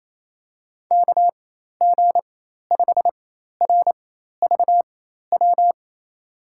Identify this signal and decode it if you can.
Morse code, heard as a keyed tone, about 28 wpm, spaced as for 13 wpm, 700 Hz. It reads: KG5RVW
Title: USA Call Signs - 4x Character Spacing - 28wpm